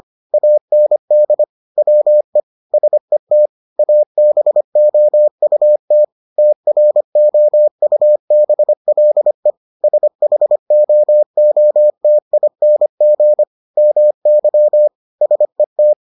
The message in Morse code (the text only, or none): AND WE SET ABOUT TROUBLE SHOOTING MY SET